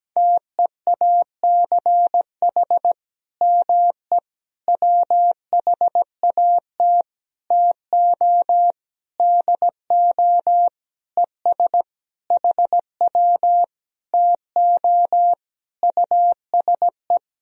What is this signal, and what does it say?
Morse code, heard as a keyed tone, about 17 wpm, 700 Hz. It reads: TEACH ME WHAT TO DO ES HW TO USE
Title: Sentences from Top 500 Words and Abbreviations 17wpm